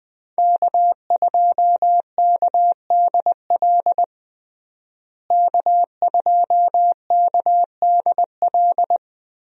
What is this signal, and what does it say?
Morse code, heard as a keyed tone, about 21 wpm, 700 Hz. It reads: K2KDL K2KDL
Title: USA Call Signs - 5 Characters - Repeated 2x - 20wpm